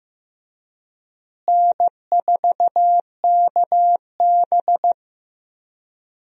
Morse code, heard as a keyed tone, about 15 wpm, 700 Hz. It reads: N4KB
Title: USA Call Signs - 4 Characters - Repeated 2x - 15wpm